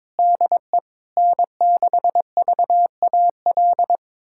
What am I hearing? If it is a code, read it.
Morse code, heard as a keyed tone, about 22 wpm, 700 Hz. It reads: DE N6VAL
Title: USA Callsign DE Callsign - 5 Characters - 22wpm